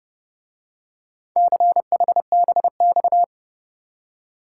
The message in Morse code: CHBX

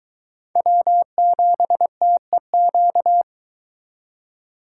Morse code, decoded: W7TEQ

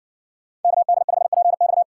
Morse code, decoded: KD5PX

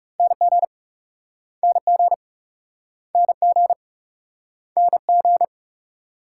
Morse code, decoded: NG NG NG NG

